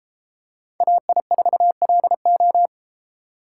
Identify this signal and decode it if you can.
Morse code, heard as a keyed tone, about 33 wpm, 700 Hz. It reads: AI4LO